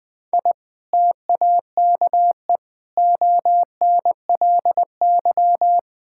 Morse code, heard as a keyed tone, about 20 wpm, 700 Hz. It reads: I TAKE ONLY
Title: Sentences from Top 1000 Words - Review Part 5 - 20wpm